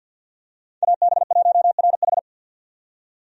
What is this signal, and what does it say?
Morse code, heard as a keyed tone, about 50 wpm, 700 Hz. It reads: AB1RH
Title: USA Call Signs - 5 Characters - 50wpm